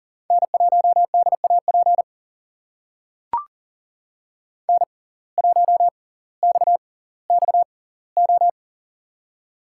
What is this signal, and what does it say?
Morse code, heard as a keyed tone, about 39 wpm, spaced as for 20 wpm, 700 Hz. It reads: N1DAP N1XXO
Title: USA Call Signs - 5 Characters - 6x Character Spacing - 40wpm